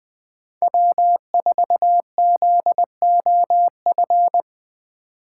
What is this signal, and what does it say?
Morse code, heard as a keyed tone, about 20 wpm, 700 Hz. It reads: W4ZOF
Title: Code Groups - 5 Letter:Numbers - 20wpm